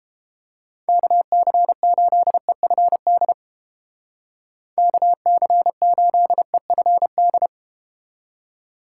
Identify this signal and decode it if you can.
Morse code, heard as a keyed tone, about 31 wpm, 700 Hz. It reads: KC8EFD KC8EFD